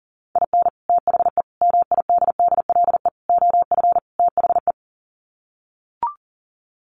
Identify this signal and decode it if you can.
Morse code, heard as a keyed tone, about 40 wpm, 700 Hz. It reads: IN THE MIDDLE OF THE
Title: Sets of 5 Words 40wpm